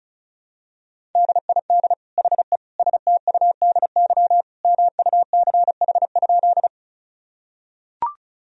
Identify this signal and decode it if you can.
Morse code, heard as a keyed tone, about 34 wpm, 700 Hz. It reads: DID HE STUDY MUCH?
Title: Sentences from Top 1000 Words - Review Part 1 - 35wpm